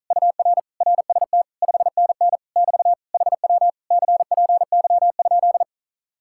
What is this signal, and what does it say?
Morse code, heard as a keyed tone, about 43 wpm, 700 Hz. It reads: UR RST 5NN <BT> HW CPY?
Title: Speed-Racing Common QSO Phrases Encore - 35wpm